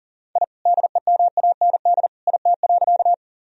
Morse code, heard as a keyed tone, about 40 wpm, 700 Hz. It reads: I DEMAND IT.
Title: Sentences from Top 1000 Words - Review Part 4 - 40wpm